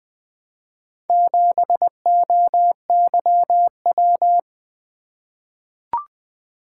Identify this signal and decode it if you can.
Morse code, heard as a keyed tone, about 20 wpm, 700 Hz. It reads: 7OYW
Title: Four Letter-Number Combinations - 20wpm